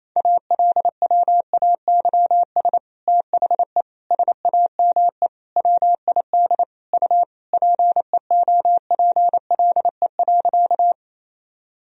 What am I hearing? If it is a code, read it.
Morse code, heard as a keyed tone, about 28 wpm, 700 Hz. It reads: ALWAYS THE SAME WID U PEOPLE.
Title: Sentences from Top 300 Words and Abbreviations 28wpm